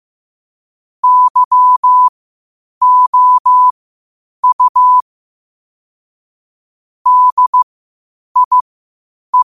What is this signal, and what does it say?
Morse code, heard as a keyed tone, about 15 wpm, spaced as for 8 wpm, 1000 Hz. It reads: YOU DIE
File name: qotdfiles / 8wpm